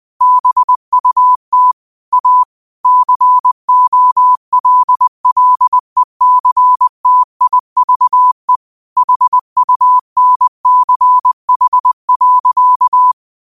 Morse code, heard as a keyed tone, about 20 wpm, 1000 Hz. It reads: BUT A COLLECTIVE HUNCH.
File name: qotdfiles / 20wpm